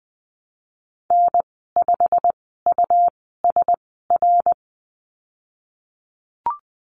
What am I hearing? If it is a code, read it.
Morse code, heard as a keyed tone, about 20 wpm, spaced as for 15 wpm, 700 Hz. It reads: N5USR
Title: 140 Lesson 23g - Letter f - Call Signs 20wpm